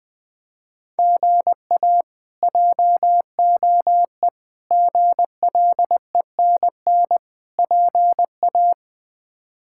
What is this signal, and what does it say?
Morse code, heard as a keyed tone, about 20 wpm, 700 Hz. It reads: GA JOE GLENN PA
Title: K1USN SST QSO Practice - 20wpm